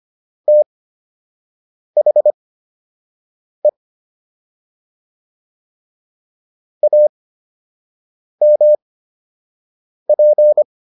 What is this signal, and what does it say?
Morse code, heard as a keyed tone, about 25 wpm, spaced as for 6 wpm, 600 Hz. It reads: THE AMP